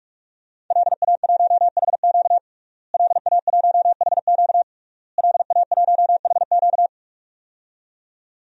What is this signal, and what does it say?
Morse code, heard as a keyed tone, about 46 wpm, 700 Hz. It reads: LA1HQ LA1HQ LA1HQ
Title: DXCC-100 Call Signs - 5 Characters - Repeated 3x - 45wpm